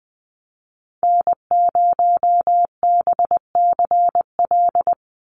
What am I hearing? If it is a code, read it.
Morse code, heard as a keyed tone, about 20 wpm, 700 Hz. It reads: N0BCL
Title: USA Call Signs 20wpm